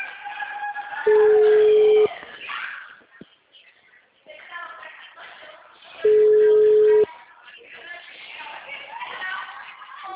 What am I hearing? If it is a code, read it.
Ringback tone (Europe, 425 Hz).